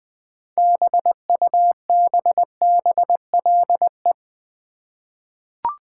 Morse code, heard as a keyed tone, about 20 wpm, 700 Hz. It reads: BUBBLE
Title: 6 Letter Words - First 3 Letters Given - 20wpm